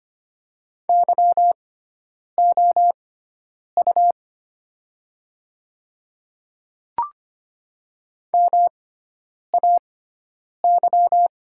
Morse code, heard as a keyed tone, about 25 wpm, spaced as for 9 wpm, 700 Hz. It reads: YOU MAY
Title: 6x Character Spacing - Top 500 Words - 25wpm